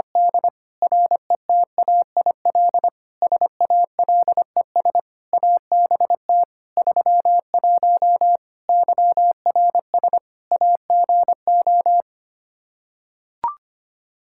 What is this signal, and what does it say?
Morse code, heard as a keyed tone, about 25 wpm, 700 Hz. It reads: D RETAIL SALES ABT 31 YRS AGO
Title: Practice QSO - Exchange 3 - More Info - 25wpm